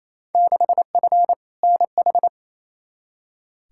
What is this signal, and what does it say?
Morse code, heard as a keyed tone, about 28 wpm, 700 Hz. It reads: 6F NH
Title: Field Day Exchanges Encore - 28wpm